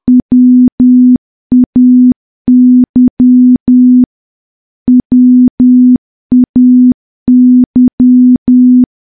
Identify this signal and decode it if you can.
Morse code, heard as a keyed tone, about 10 wpm, 250 Hz. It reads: WAY WAY
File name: morse / morse-training-10WPM